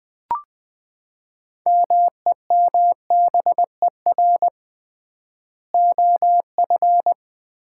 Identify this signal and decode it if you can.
Morse code, heard as a keyed tone, about 20 wpm, 700 Hz. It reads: MEMBER OF
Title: 3x Word-Spacing - Sets of 3 Words - 20wpm